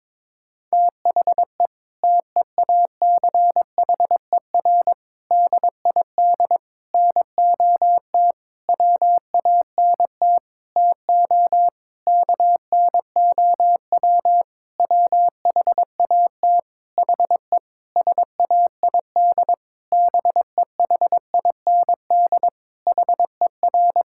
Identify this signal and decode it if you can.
Morse code, heard as a keyed tone, about 22 wpm, 700 Hz. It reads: THE TEACHER DID NOT WANT TO KNOW WHAT HE SAID BEHIND HER
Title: Sentences from Top 1000 Words - Review Part 6 - 22wpm